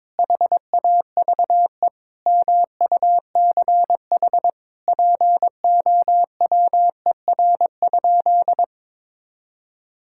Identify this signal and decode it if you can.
Morse code, heard as a keyed tone, about 22 wpm, 700 Hz. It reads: HAVE MUCH POWER?
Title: Sentences from Top 1000 Words - Review Part 5 - 22wpm